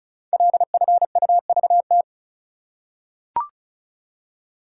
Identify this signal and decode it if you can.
Morse code, heard as a keyed tone, about 36 wpm, 700 Hz. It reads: LFUVT
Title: Code Groups - 5 Letters - 35wpm